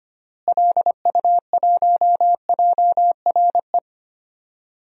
Morse code, heard as a keyed tone, about 25 wpm, 700 Hz. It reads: LU1JRE
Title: DXCC-100 Call Signs - 6 Characters - Repeated 3x - 25wpm